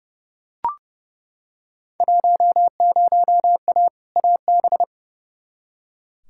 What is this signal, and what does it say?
Morse code, heard as a keyed tone, about 32 wpm, 700 Hz. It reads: 10A AB
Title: Field Day Exchanges Encore - 30wpm